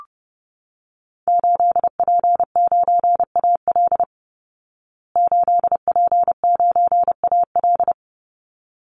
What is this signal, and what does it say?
Morse code, heard as a keyed tone, about 30 wpm, 700 Hz. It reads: 8P9AL 8P9AL
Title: Call Signs Repeated 3x - 30wpm